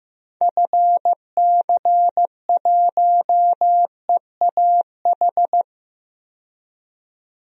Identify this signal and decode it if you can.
Morse code, heard as a keyed tone, about 15 wpm, 700 Hz. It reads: FC1EAH
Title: DXCC-100 Call Signs - 6 Characters - 15wpm